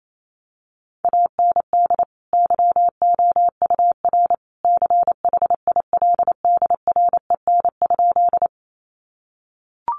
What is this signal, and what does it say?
Morse code, heard as a keyed tone, about 28 wpm, 700 Hz. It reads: AND YOUR CHILDREN?
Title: Sentences from Top 300 Words 28wpm